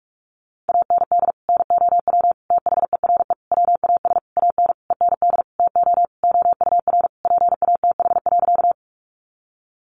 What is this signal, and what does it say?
Morse code, heard as a keyed tone, about 44 wpm, 700 Hz. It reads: AND NOW THERE WAS AN END TO OUR PATH.
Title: Sentences from Top 1400 Words 45wpm